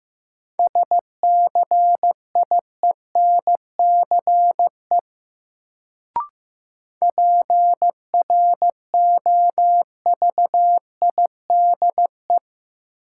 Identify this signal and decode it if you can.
Morse code, heard as a keyed tone, about 15 wpm, 700 Hz. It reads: SCIENCE PROVIDE
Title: Sentences from Top 1000 Words - Review Part 2 - New Word Drill - 15wpm